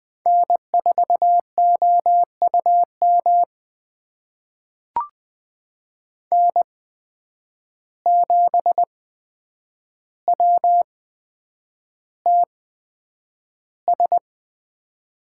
Morse code, heard as a keyed tone, about 20 wpm, spaced as for 9 wpm, 700 Hz. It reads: N4OUM N7WTS
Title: USA Call Signs - 5 Characters - 8x Character Spacing - 20wpm